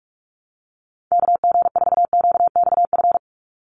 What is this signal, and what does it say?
Morse code, heard as a keyed tone, about 44 wpm, 700 Hz. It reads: KG4QXF
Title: USA Call Signs 45wpm